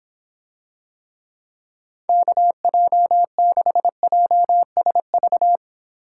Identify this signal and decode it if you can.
Morse code, heard as a keyed tone, about 26 wpm, 700 Hz. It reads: KJ6JSV